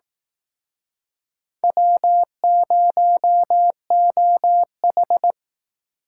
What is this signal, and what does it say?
Morse code, heard as a keyed tone, about 18 wpm, 700 Hz. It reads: W0OH
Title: Speed Racing - USA 1x2 Call Signs 15wpm